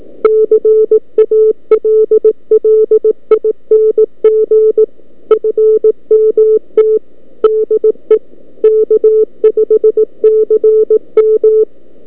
Morse code, heard as a keyed tone, about 18 wpm, 440 Hz. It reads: CALLING FMT DE K5CM
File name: K5CM 20m-9-11-11by-abi